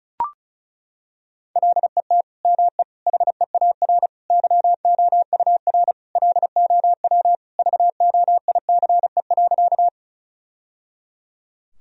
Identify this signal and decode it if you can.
Morse code, heard as a keyed tone, about 36 wpm, 700 Hz. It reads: LET ME HEAR YOUR LOW VOICE.
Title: Sentences from Top 1000 Words - Review Part 1 - 35wpm